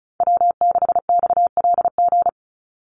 Morse code, heard as a keyed tone, about 34 wpm, 700 Hz. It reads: W6XLG
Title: USA Call Signs 35wpm